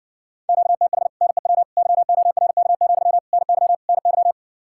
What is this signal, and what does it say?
Morse code, heard as a keyed tone, about 60 wpm, 700 Hz. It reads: QTH NEW YORK, NY NY